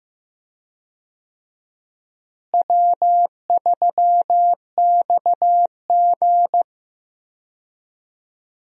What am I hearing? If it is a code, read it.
Morse code, heard as a keyed tone, about 15 wpm, 700 Hz. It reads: W3XG
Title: USA Call Signs - 4 Characters - Repeated 2x - 15wpm